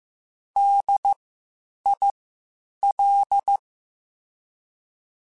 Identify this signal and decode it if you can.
Morse code, heard as a keyed tone, about 15 wpm, spaced as for 8 wpm, 800 Hz. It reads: DIL
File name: Spaced Alphas / Spaced Alphas mp3s / Ds sp @13